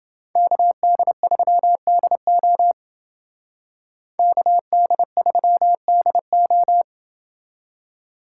Morse code, heard as a keyed tone, about 29 wpm, 700 Hz. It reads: KD3DO KD3DO